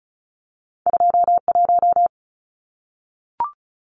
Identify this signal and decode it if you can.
Morse code, heard as a keyed tone, about 36 wpm, 700 Hz. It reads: 21
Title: Numbers - 2-Digit 35wpm